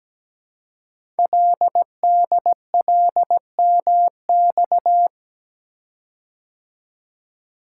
Morse code, heard as a keyed tone, about 17 wpm, 700 Hz. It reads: LDLMX
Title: Code Groups - 5 Letters - 17wpm